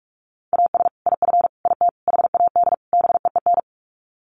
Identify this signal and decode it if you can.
Morse code, heard as a keyed tone, about 44 wpm, 700 Hz. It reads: AS IF IT HAD BEEN
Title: Sets of 5 Words Encore2 45wpm